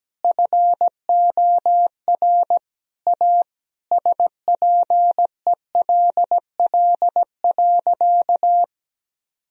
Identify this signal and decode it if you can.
Morse code, heard as a keyed tone, about 17 wpm, 700 Hz. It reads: FOR A SPELL.
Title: Sentences from Top 1000 Words - Review Part 1 - 17wpm